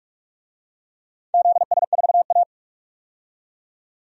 Morse code, heard as a keyed tone, about 42 wpm, 700 Hz. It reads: ZS4A
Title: DXCC-100 Call Signs - 4 Characters - 45wpm